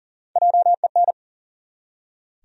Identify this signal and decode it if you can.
Morse code, heard as a keyed tone, about 39 wpm, 700 Hz. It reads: JEN
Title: USA Common Names - 40wpm